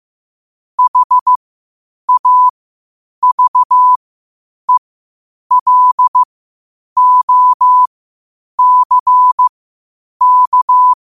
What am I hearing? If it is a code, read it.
Morse code, heard as a keyed tone, about 15 wpm, spaced as for 8 wpm, 1000 Hz. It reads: HAVELOCK
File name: qotdfiles / 8wpm